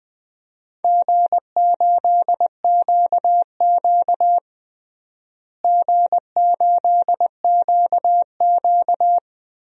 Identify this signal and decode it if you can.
Morse code, heard as a keyed tone, about 20 wpm, 700 Hz. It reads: G8QQ G8QQ
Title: DXCC-100 Call Signs - 4 Characters - Repeated 2x - 20wpm